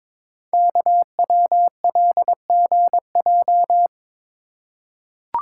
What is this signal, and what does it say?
Morse code, heard as a keyed tone, about 22 wpm, 700 Hz. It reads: KWLGJ
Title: Code Groups - 5 Letters - 22wpm